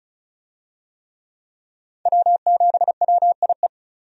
Morse code, heard as a keyed tone, about 36 wpm, 700 Hz. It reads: W7WIE
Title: USA Call Signs - 5 Characters - 35wpm